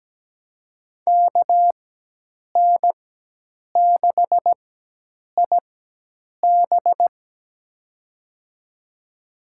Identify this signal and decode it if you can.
Morse code, heard as a keyed tone, about 17 wpm, spaced as for 8 wpm, 700 Hz. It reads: KN6IB
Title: USA Call Signs - 5 Characters - 4x Character Spacing - 17wpm